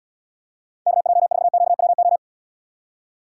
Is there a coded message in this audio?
Yes: Morse code from 0.9 to 2.2 s, heard as a keyed tone, about 75 wpm, 700 Hz. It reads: KJ4QWO